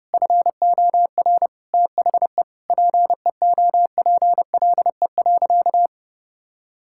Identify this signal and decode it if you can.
Morse code, heard as a keyed tone, about 30 wpm, 700 Hz. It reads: FOR THE PEOPLE.
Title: Sentences from Top 1000 Words - Review Part 5 - 30wpm